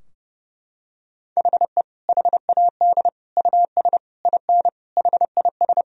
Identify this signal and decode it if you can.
Morse code, heard as a keyed tone, about 30 wpm, 700 Hz. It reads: HE HAD US IN HIS
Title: Sentences from Top 1000 Words - Review Part 2 - 30wpm